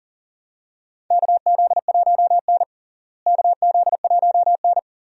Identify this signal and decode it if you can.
Morse code, heard as a keyed tone, about 40 wpm, 700 Hz. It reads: KZ1N KZ1N
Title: USA Call Signs - 4 Characters - Repeated 2x - 40wpm